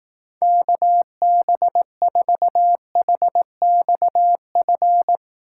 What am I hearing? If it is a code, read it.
Morse code, heard as a keyed tone, about 18 wpm, 700 Hz. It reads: KB4HXF
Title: Speed Racing - USA 2x3 Call Signs 15wpm